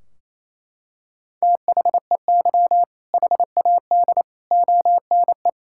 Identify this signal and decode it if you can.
Morse code, heard as a keyed tone, about 28 wpm, 700 Hz. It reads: THEY HAD ONE
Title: Sentences from Top 1600 Words 28wpm